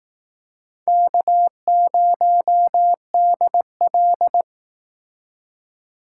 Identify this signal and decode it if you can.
Morse code, heard as a keyed tone, about 18 wpm, 700 Hz. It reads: K0DL
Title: Speed Racing - USA 1x2 Call Signs 15wpm